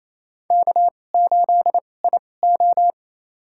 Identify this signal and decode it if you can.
Morse code, heard as a keyed tone, about 27 wpm, spaced as for 20 wpm, 700 Hz. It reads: K8IO
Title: USA Call Signs - 4 Characters - 2x Character Spacing - 28wpm